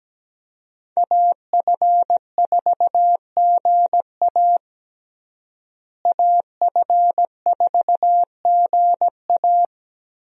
Morse code, heard as a keyed tone, about 17 wpm, 700 Hz. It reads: AF4GA AF4GA
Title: USA Call Signs - 5 Characters - Repeated 2x - 17wpm